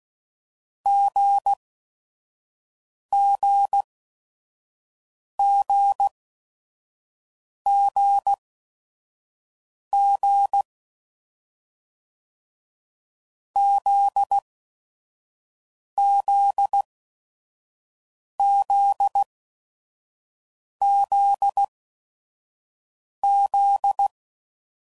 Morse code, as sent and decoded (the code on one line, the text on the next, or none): --. --. --. --. --. / --.. --.. --.. --.. --..
GGGGG ZZZZZ